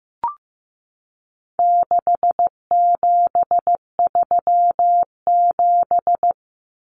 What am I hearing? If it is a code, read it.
Morse code, heard as a keyed tone, about 15 wpm, 700 Hz. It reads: 6737
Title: Common Cut Numbers - 4-Digit 15wpm